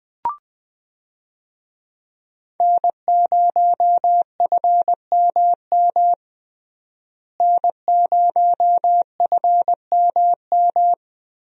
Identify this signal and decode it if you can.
Morse code, heard as a keyed tone, about 21 wpm, 700 Hz. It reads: N0FMM N0FMM
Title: USA Call Signs - 5 Characters - Repeated 3x - 20wpm